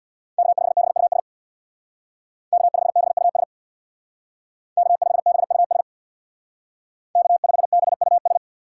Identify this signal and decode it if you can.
Morse code, heard as a keyed tone, about 59 wpm, 700 Hz. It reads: K5BUS K5BUS K5BUS K5BUS
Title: Speed Racing - USA 1x3 Call Signs 50wpm